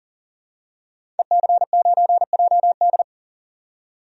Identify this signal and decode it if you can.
Morse code, heard as a keyed tone, about 40 wpm, 700 Hz. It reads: EC9JD
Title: DXCC-100 Call Signs - 5 Characters - 40wpm